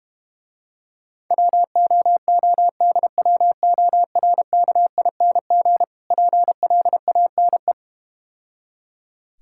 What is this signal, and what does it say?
Morse code, heard as a keyed tone, about 32 wpm, 700 Hz. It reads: WOODWORKING PLANE